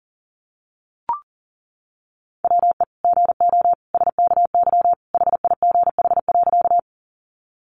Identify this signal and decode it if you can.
Morse code, heard as a keyed tone, about 40 wpm, 700 Hz. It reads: WE GO SKY HIGH.
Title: Sentences from Top 1200 Words 40wpm